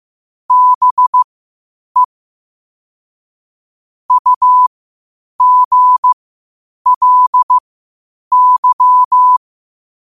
Morse code, heard as a keyed tone, about 15 wpm, spaced as for 8 wpm, 1000 Hz. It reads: BE UGLY
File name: qotdfiles / 8wpm